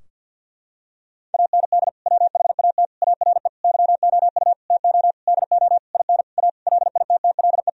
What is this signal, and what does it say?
Morse code, heard as a keyed tone, about 47 wpm, 700 Hz. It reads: AND WHAT ARE YOU TO DO IN A LITTLE
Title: Sentences from Top 1000 Words - Review Part 2 - 50wpm